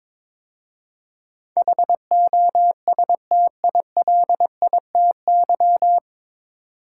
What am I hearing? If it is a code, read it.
Morse code, heard as a keyed tone, about 22 wpm, 700 Hz. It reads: HOSTILITY
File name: 9 Letter Words - First 3 Letters Given - 22wpm